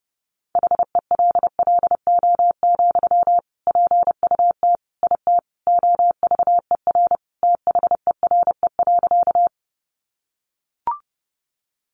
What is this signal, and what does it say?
Morse code, heard as a keyed tone, about 30 wpm, 700 Hz. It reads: HELLO, PUT IT OVER THERE.
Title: Sentences from Top 1100 Words 30wpm